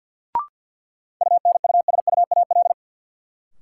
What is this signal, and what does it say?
Morse code, heard as a keyed tone, about 49 wpm, 700 Hz. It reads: UNUSUAL
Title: Sentences from Top 1400 Words - New Word Drill - 50wpm